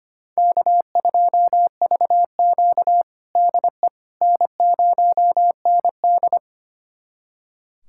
Morse code, heard as a keyed tone, about 25 wpm, 700 Hz. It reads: K2VQ DE N0ND
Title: USA Callsign DE Callsign - 4 Characters - 25wpm